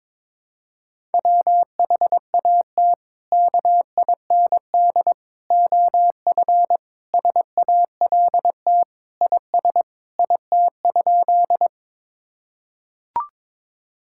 Morse code, heard as a keyed tone, about 22 wpm, 700 Hz. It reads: WHAT KIND OF SALT IS IT?
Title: Sentences from Top 1000 Words - Review Part 5 - 22wpm